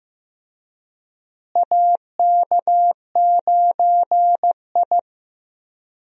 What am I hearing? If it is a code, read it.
Morse code, heard as a keyed tone, about 15 wpm, 700 Hz. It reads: AK9I
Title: USA Call Signs - 4 Characters - Repeated 3x - 15wpm